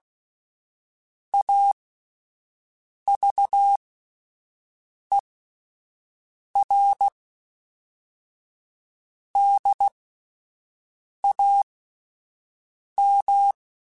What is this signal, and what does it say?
Morse code, heard as a keyed tone, about 16 wpm, spaced as for 5 wpm, 800 Hz. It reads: AVERDAM